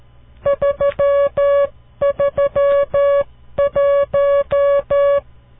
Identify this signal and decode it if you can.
Morse code, heard as a keyed tone, about 13 wpm, 560 Hz. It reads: 331
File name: CW Learning Programs / farnsworth / unit 5